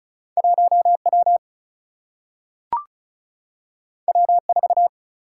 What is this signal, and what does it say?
Morse code, heard as a keyed tone, about 36 wpm, 700 Hz. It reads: 1W W4
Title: Single-Letter Single-Number 35wpm